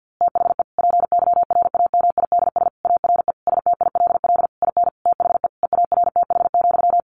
Morse code, heard as a keyed tone, about 50 wpm, 700 Hz. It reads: THE PYRAMIDS ARE STILL IN THE EARTH,
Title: A Sound Of Thunder 50wpm